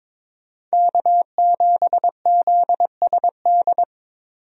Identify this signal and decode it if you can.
Morse code, heard as a keyed tone, about 22 wpm, 700 Hz. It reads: K7ZSD
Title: USA Call Signs - 5 Characters - 22wpm